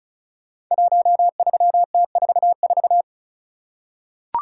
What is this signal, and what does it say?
Morse code, heard as a keyed tone, about 36 wpm, 690 Hz. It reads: 13T44
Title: Code Groups - 5 Letter:Numbers - 35wpm